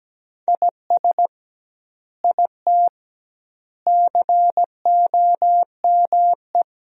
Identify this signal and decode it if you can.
Morse code, heard as a keyed tone, about 17 wpm, 700 Hz. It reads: IS IT COME
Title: 2x Word-Spacing - Sentences from Top 500 Words - 17wpm